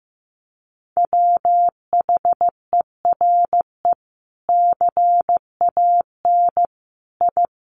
Morse code, heard as a keyed tone, about 15 wpm, 700 Hz. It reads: WHERE CAN I
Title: Sentences from Top 1500 Words 15wpm